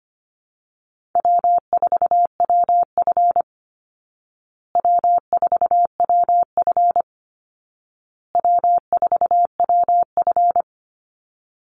Morse code, heard as a keyed tone, about 25 wpm, 700 Hz. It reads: W4WF W4WF W4WF
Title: Easy Call Signs Repeated 3x - 25wpm